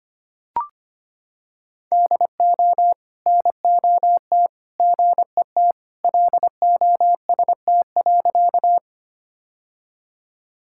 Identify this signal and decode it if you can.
Morse code, heard as a keyed tone, about 25 wpm, 700 Hz. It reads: DO NOT GET LOST.
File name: practice / 25wpm / Sentences with 3-Word Phrases - 25wpm